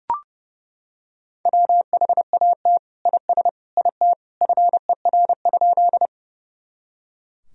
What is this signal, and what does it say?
Morse code, heard as a keyed tone, about 30 wpm, 700 Hz. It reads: WHAT IS IT FER?
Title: Sentences from Top 100 Words and Abbreviations 30wpm